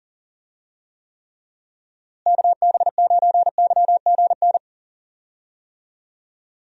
Morse code, heard as a keyed tone, about 40 wpm, 700 Hz. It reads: KB9YGN
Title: USA Call Signs - 6 Characters - Repeated 3x - 40wpm